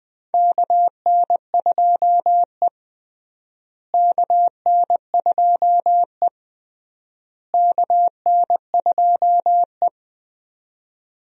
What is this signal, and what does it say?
Morse code, heard as a keyed tone, about 21 wpm, 700 Hz. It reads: KN2E KN2E KN2E
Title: USA Call Signs - 4 Characters - Repeated 3x - 20wpm